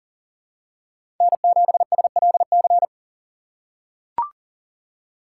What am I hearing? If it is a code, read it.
Morse code, heard as a keyed tone, about 39 wpm, 700 Hz. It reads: N7SLC
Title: USA Call Signs - 5 Characters - 4x Character Spacing - 40wpm